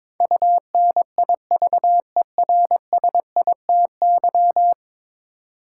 Morse code, heard as a keyed tone, about 22 wpm, 700 Hz. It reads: UNIVERSITY
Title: Sentences from Top 1000 Words - Review Part 4 - New Word Drill - 22wpm